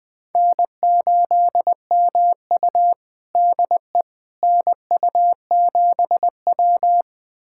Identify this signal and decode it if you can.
Morse code, heard as a keyed tone, about 20 wpm, 700 Hz. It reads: N8MU DE NU7W
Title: USA Callsign DE Callsign - 4 Characters - 20wpm